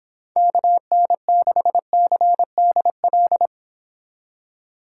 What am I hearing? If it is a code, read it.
Morse code, heard as a keyed tone, about 26 wpm, 700 Hz. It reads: KN6CDL